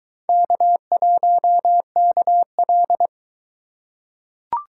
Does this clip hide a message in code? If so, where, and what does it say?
Yes: Morse code from 0.3 to 3.1 s, heard as a keyed tone, about 21 wpm, 700 Hz. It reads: K1KL